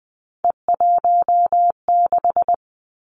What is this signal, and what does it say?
Morse code, heard as a keyed tone, about 20 wpm, 700 Hz. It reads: E16
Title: Three Challenging Letter-Number Combinations 20wpm